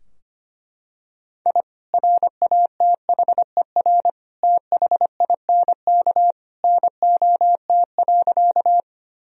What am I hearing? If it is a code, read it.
Morse code, heard as a keyed tone, about 25 wpm, 700 Hz. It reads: I RATHER THINK NOT.
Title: Sentences from Top 1000 Words - Review Part 4 - 25wpm